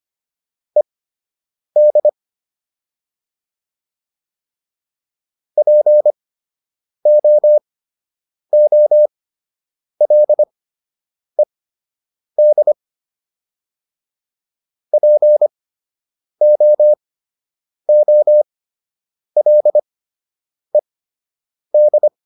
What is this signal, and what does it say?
Morse code, heard as a keyed tone, about 25 wpm, spaced as for 8 wpm, 600 Hz. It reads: ED POOLED POOLED
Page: fundamental-practice-files